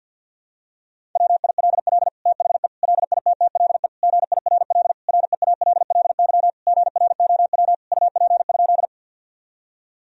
Morse code, heard as a keyed tone, about 48 wpm, 700 Hz. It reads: WILL THE LITTLE GIRL REALLY GROW UP?
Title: Sentences from Top 1000 Words - Review Part 1 - 50wpm